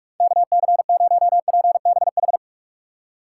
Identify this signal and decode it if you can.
Morse code, heard as a keyed tone, about 46 wpm, 700 Hz. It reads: KC0PBH
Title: USA Call Signs - 6 Characters - 4x Character Spacing - 45wpm